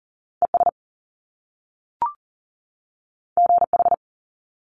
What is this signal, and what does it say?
Morse code, heard as a keyed tone, about 42 wpm, 700 Hz. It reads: ES GH
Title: Top Two Letter Combinations in Words 40wpm